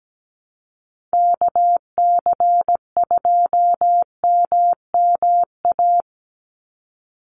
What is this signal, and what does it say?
Morse code, heard as a keyed tone, about 17 wpm, 700 Hz. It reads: KC2MMA
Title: USA Call Signs 17wpm